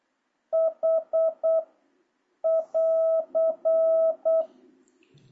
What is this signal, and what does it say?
Morse code, heard as a keyed tone, about 8 wpm, 650 Hz. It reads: H <AR>
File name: jh8cbh / 2026cw